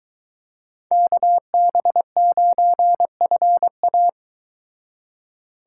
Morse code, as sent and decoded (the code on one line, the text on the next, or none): -.- -... ----. ..-. .-
KB9FA